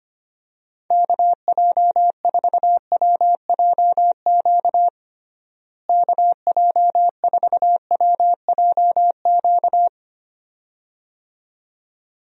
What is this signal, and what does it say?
Morse code, heard as a keyed tone, about 25 wpm, 700 Hz. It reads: KJ4WJQ KJ4WJQ
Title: USA Call Signs - 6 Characters - Repeated 2x - 25wpm